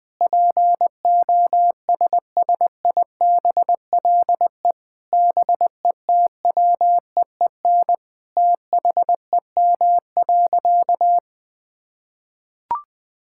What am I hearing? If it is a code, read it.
Morse code, heard as a keyed tone, about 21 wpm, 700 Hz. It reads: POSSIBLE BETWEEN THEM.
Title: Sentences from Top 1000 Words - Review Part 5 - 20wpm